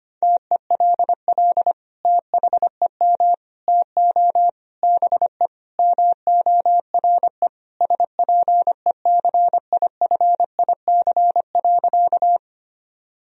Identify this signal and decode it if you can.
Morse code, heard as a keyed tone, about 25 wpm, 700 Hz. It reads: TELL THEM TO BE MORE SPECIFIC.
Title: Sentences from Top 1000 Words - Review Part 3 - 25wpm